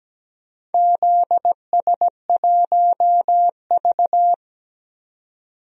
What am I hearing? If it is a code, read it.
Morse code, heard as a keyed tone, about 17 wpm, 700 Hz. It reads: ZS1V
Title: DXCC-100 Call Signs - 4 Characters - 17wpm